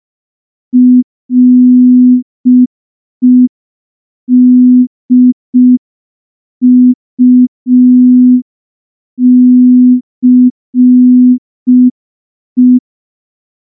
Morse code, heard as a keyed tone, about 5 wpm, 250 Hz. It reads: REDUCE
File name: morse / morse-training-05WPM